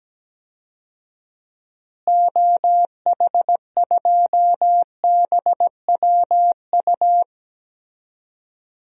Morse code, heard as a keyed tone, about 17 wpm, 700 Hz. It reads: OH2BWU
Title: DXCC-100 Call Signs - 6 Characters - Repeated 2x - 17wpm